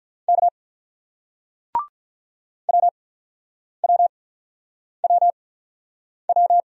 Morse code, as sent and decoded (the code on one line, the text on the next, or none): -.- / .-- .-- .-- .--
K WWWW